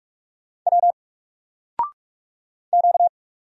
Morse code, heard as a keyed tone, about 44 wpm, 700 Hz. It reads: W Q
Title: Single Letter-Number 45wpm